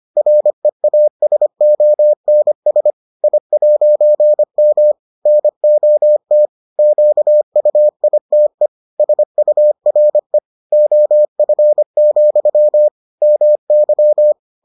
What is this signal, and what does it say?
Morse code, heard as a keyed tone, about 25 wpm, 600 Hz. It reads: REASONS I'M NOT QUITE SURE OF, MY